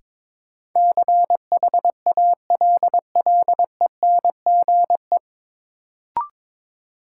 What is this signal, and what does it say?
Morse code, heard as a keyed tone, about 21 wpm, 700 Hz. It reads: CHALLENGE
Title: Sentences from Top 1000 Words - Review Part 7 - New Word Drill - 22wpm